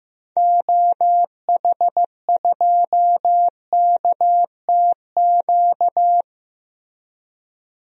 Morse code, heard as a keyed tone, about 15 wpm, 700 Hz. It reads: OH2KTQ
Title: DXCC-100 Call Signs - 6 Characters - 15wpm